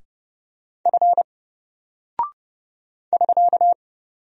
Morse code, heard as a keyed tone, about 30 wpm, 700 Hz. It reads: F <SK>
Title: Single Character (Letter:Number:Punctuation:Prosign) 30wpm